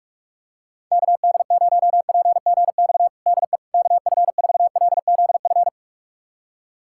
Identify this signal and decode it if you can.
Morse code, heard as a keyed tone, about 46 wpm, 700 Hz. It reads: KD0PGX DE KF4LZF
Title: USA Callsign DE Callsign 45wpm